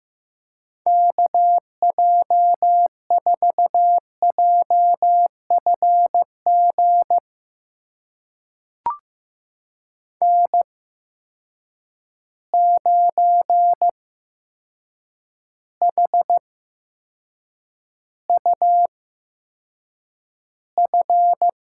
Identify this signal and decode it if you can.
Morse code, heard as a keyed tone, about 15 wpm, spaced as for 7 wpm, 700 Hz. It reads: KJ4JFG N9HUF
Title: USA Call Signs - 8x Character Spacing - 15wpm